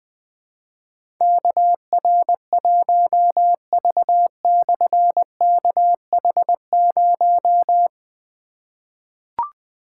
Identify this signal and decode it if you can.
Morse code, heard as a keyed tone, about 20 wpm, 700 Hz. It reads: KR1V/KH0
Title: Difficult Call Signs 20wpm